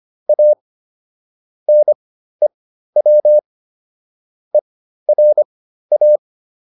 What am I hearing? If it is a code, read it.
Morse code, heard as a keyed tone, about 25 wpm, spaced as for 13 wpm, 600 Hz. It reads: A NEW ERA